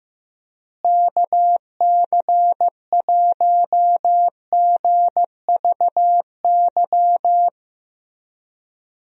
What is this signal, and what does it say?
Morse code, heard as a keyed tone, about 15 wpm, 700 Hz. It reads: KC1GVY
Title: USA Call Signs - 6 Characters - Repeated 2x - 15wpm